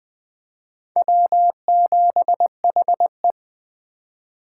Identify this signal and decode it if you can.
Morse code, heard as a keyed tone, about 20 wpm, 700 Hz. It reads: W7HE
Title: USA Call Signs - 4 Characters - 3x Character Spacing - 20wpm